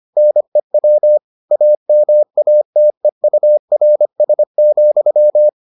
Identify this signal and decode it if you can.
Morse code, heard as a keyed tone, about 25 wpm, 600 Hz. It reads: NEW AMATEURS,